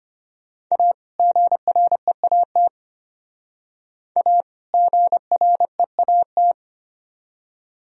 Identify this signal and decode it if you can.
Morse code, heard as a keyed tone, about 33 wpm, 700 Hz. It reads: A GREAT A GREAT